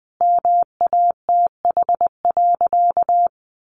Morse code, heard as a keyed tone, about 20 wpm, 700 Hz. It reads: MATH.
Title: Easy Phrases 20wpm